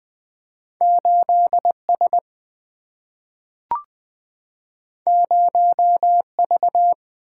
Morse code, heard as a keyed tone, about 20 wpm, 690 Hz. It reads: 8S 0V
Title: Two Letter-Number Combinations - 20wpm